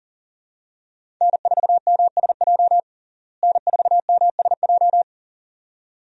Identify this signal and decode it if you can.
Morse code, heard as a keyed tone, about 40 wpm, 700 Hz. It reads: N4MSJ N4MSJ
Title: USA Call Signs - 5 Characters - Repeated 2x - 40wpm